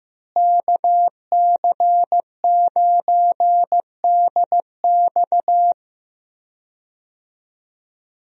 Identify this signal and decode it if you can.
Morse code, heard as a keyed tone, about 15 wpm, 700 Hz. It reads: KC9DX
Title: USA Call Signs - 5 Characters - Repeated 2x - 15wpm